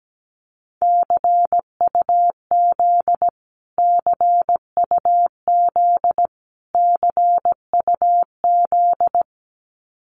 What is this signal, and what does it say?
Morse code, heard as a keyed tone, about 17 wpm, 700 Hz. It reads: CUZ CUZ CUZ
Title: Abbreviations Repeated 3x - 17wpm